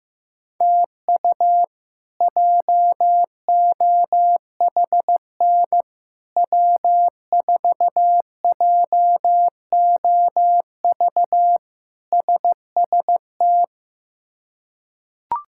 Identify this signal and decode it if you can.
Morse code, heard as a keyed tone, about 15 wpm, 700 Hz. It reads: TU JOHN W4JOV SST
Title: K1USN SST QSO Practice - 15wpm